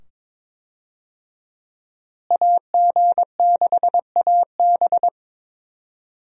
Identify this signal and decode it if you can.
Morse code, heard as a keyed tone, about 22 wpm, 700 Hz. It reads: AG6AB
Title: USA Call Signs - 5 Characters - Repeated 2x - 22wpm